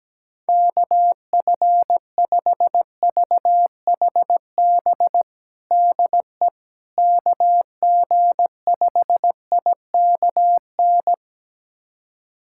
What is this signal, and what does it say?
Morse code, heard as a keyed tone, about 17 wpm, 700 Hz. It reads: KF5VHB DE KG5IKN
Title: USA Callsign DE Callsign - 6 Characters - 17wpm